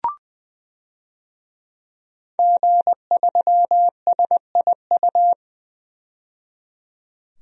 Morse code, heard as a keyed tone, about 20 wpm, 700 Hz. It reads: G3SIU
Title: DXCC-100 Call Signs - 5 Characters - 20wpm